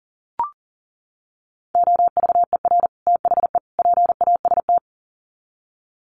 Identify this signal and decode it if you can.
Morse code, heard as a keyed tone, about 40 wpm, 700 Hz. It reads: OVER THE PAST
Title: Sets of 3 Words 40wpm